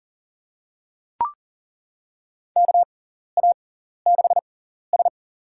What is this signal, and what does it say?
Morse code, heard as a keyed tone, about 40 wpm, spaced as for 14 wpm, 700 Hz. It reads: KA6S
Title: USA Call Signs - 4 Characters - 6x Character Spacing - 40wpm